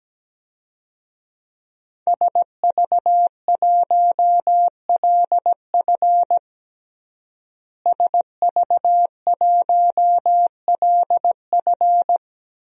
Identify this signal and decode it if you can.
Morse code, heard as a keyed tone, about 17 wpm, 700 Hz. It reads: SV1LF SV1LF
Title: DXCC-100 Call Signs - 5 Characters - Repeated 3x - 17wpm